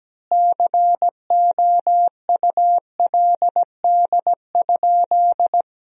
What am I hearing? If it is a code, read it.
Morse code, heard as a keyed tone, about 17 wpm, 700 Hz. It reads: COULD?
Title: 3x Word-Spacing - Sentences from Top 400 Words - 17wpm